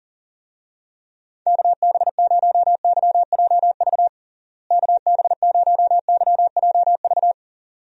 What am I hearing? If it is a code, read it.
Morse code, heard as a keyed tone, about 40 wpm, 700 Hz. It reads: KB0YJV KB0YJV
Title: USA Call Signs - 6 Characters - Repeated 2x - 40wpm